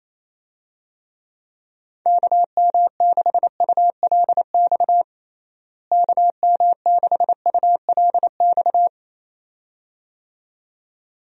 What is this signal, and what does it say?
Morse code, heard as a keyed tone, about 28 wpm, 700 Hz. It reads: KM6ULX KM6ULX
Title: USA Call Signs - 6 Characters - Repeated 2x - 28wpm